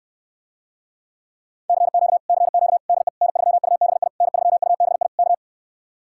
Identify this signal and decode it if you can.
Morse code, heard as a keyed tone, about 69 wpm, 700 Hz. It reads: CQ CQ DE N3UBI N3UBI K